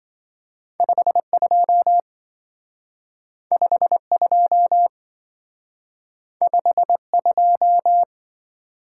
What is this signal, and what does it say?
Morse code, heard as a keyed tone, about 24 wpm, 700 Hz. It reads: 52 52 52
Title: Speed-Racing - Two Letter-Number Combinations - 20wpm